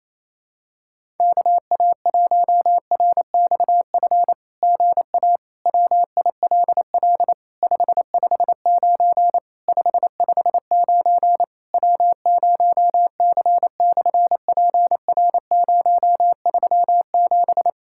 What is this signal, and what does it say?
Morse code, heard as a keyed tone, about 28 wpm, 700 Hz. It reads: KA1RXF GA WILL 559 559 W0C/PR037
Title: SOTA QSO Practice 28wpm